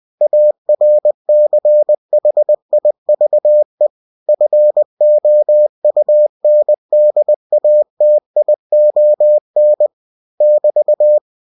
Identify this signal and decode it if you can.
Morse code, heard as a keyed tone, about 20 wpm, 600 Hz. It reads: ARCHIVE FOUNDATION <BT>
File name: cw-books / winnie-the-pooh / winnie-20wpm-page-0120